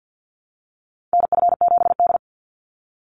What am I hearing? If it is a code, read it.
Morse code, heard as a keyed tone, about 50 wpm, 700 Hz. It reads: NF7D
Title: Call Signs - 50wpm